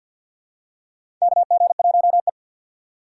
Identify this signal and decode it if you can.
Morse code, heard as a keyed tone, about 48 wpm, 700 Hz. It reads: KG1E
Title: USA Call Signs - 4 Characters - 6x Character Spacing - 50wpm